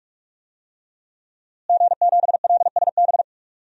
Morse code, heard as a keyed tone, about 46 wpm, 700 Hz. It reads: G7LSB
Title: DXCC-100 Call Signs - 5 Characters - Repeated 2x - 45wpm